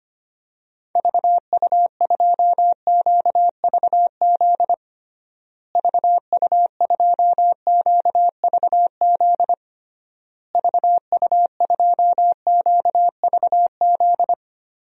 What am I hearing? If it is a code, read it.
Morse code, heard as a keyed tone, about 25 wpm, 700 Hz. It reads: VU2QVZ VU2QVZ VU2QVZ
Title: DXCC-100 Call Signs - 6 Characters - Repeated 3x - 25wpm